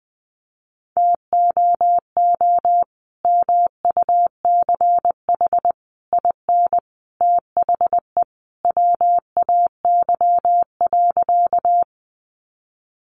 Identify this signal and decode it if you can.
Morse code, heard as a keyed tone, about 20 wpm, 700 Hz. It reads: TOO MUCH IN THE WAY.
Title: Sentences from Top 200 Words 20wpm